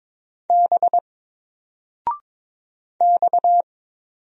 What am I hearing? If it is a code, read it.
Morse code, heard as a keyed tone, about 22 wpm, 700 Hz. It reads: B X
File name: Single-Letter-Number-22wpm